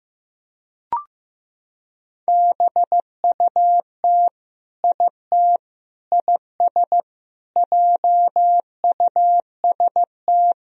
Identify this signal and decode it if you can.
Morse code, heard as a keyed tone, about 15 wpm, 700 Hz. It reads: BUT IT IS JUST
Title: Sentences from Top 200 Words and Abbreviations 15wpm